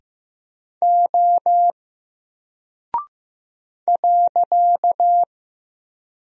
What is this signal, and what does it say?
Morse code, heard as a keyed tone, about 15 wpm, 700 Hz. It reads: O .
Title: Single Character (Letter:Number:Punctuation:Prosign) 15wpm